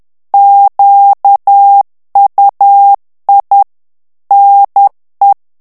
Morse code, heard as a keyed tone, about 10 wpm, 800 Hz. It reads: QUI NE
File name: FRANCAIS TEXTE CLAIR 2 VITESSE 10 MOTS MINUTES